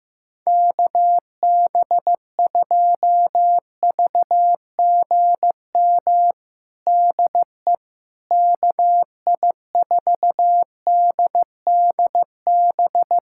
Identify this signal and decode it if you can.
Morse code, heard as a keyed tone, about 15 wpm, 700 Hz. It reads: KB2VGM DE KI4DDB
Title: USA Callsign DE Callsign - 6 Characters - 15wpm